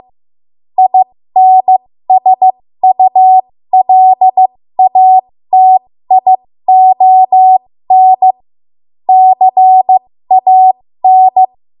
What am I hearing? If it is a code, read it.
Morse code, heard as a keyed tone, about 15 wpm, spaced as for 13 wpm, 750 Hz. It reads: INSULATION CAN
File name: Morse / 13WPM